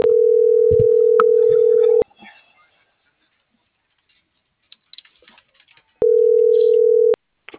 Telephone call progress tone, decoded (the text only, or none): ringback tone